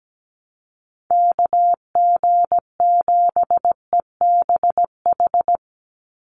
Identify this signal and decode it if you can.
Morse code, heard as a keyed tone, about 17 wpm, 700 Hz. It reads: KG7EBH
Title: USA Call Signs 17wpm